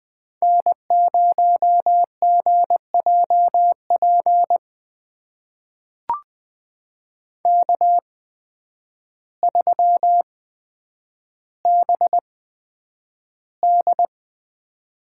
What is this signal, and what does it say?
Morse code, heard as a keyed tone, about 20 wpm, spaced as for 9 wpm, 700 Hz. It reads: N0GJP K3BD
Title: USA Call Signs - 5 Characters - 8x Character Spacing - 20wpm